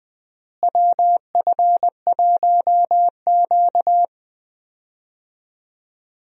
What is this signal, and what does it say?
Morse code, heard as a keyed tone, about 21 wpm, 700 Hz. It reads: WF1Q
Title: USA Call Signs - 4 Characters - Repeated 3x - 20wpm